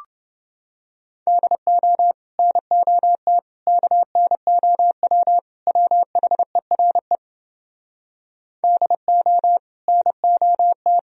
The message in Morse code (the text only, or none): DO NOT KNOW WHERE DO NOT